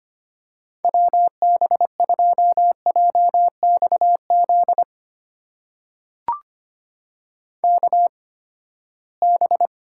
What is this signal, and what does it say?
Morse code, heard as a keyed tone, about 25 wpm, spaced as for 17 wpm, 700 Hz. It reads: WB2JXZ KB
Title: USA Call Signs - 6 Characters - 8x Character Spacing - 25wpm